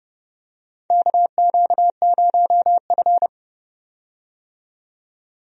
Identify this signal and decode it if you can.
Morse code, heard as a keyed tone, about 31 wpm, 700 Hz. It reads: KQ0F
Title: USA Call Signs - 4 Characters - 30wpm